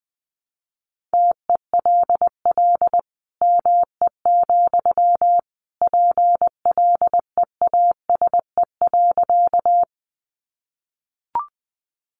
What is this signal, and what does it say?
Morse code, heard as a keyed tone, about 20 wpm, 700 Hz. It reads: TELL ME, PLEASE.
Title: Sentences from Top 1500 Words 20wpm